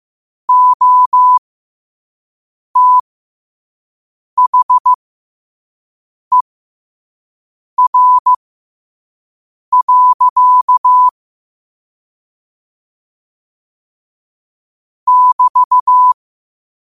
Morse code, heard as a keyed tone, about 15 wpm, spaced as for 5 wpm, 1000 Hz. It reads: OTHER. <BT>
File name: qotdfiles / 5wpm